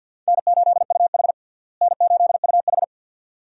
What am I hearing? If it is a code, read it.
Morse code, heard as a keyed tone, about 53 wpm, 700 Hz. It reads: N8UH N8UH
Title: USA Call Signs - 4 Characters - Repeated 2x - 50wpm